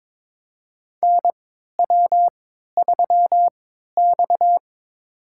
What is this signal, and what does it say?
Morse code, heard as a keyed tone, about 23 wpm, spaced as for 13 wpm, 700 Hz. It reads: NW3X
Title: USA Call Signs - 4 Characters - 3x Character Spacing - 22wpm